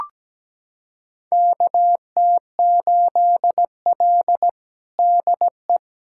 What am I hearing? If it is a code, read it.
Morse code, heard as a keyed tone, about 17 wpm, 700 Hz. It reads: KT8L DE
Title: USA Callsign DE Callsign - 4 Characters - 17wpm